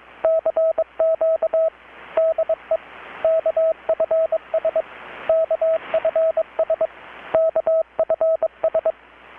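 Morse code, heard as a keyed tone, about 22 wpm, 650 Hz, with radio noise. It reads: CQ DE KFS KFS KFS
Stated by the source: Morse from KPH's Night of Nights XX maritime radiotelegraphy event, received in Chiapas, Mexico about 3,800 km away on a Kenwood R-600 with a grounded 10 m long-wire antenna.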